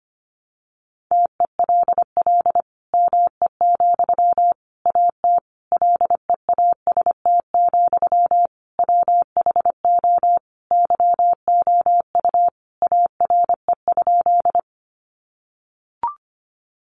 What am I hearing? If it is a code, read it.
Morse code, heard as a keyed tone, about 25 wpm, 700 Hz. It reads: TELL ME, AT LEAST, WHO YOU ARE?
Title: Sentences from Top 700 Words 25wpm